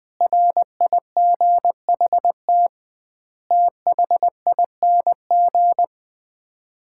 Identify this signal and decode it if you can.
Morse code, heard as a keyed tone, about 21 wpm, 700 Hz. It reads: RIGHT THING
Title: 2x Word-Spacing - Sets of 5 Words - 20wpm